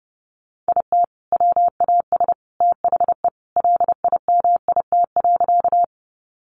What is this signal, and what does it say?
Morse code, heard as a keyed tone, about 30 wpm, 700 Hz. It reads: IT WAS THE LIMIT.
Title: Sentences from Top 1500 Words 30wpm